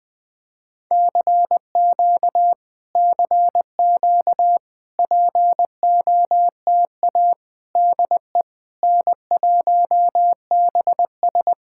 Morse code, heard as a keyed tone, about 20 wpm, 700 Hz. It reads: CQ CQ POTA DE N1BS
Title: POTA QSO Practice - 20wpm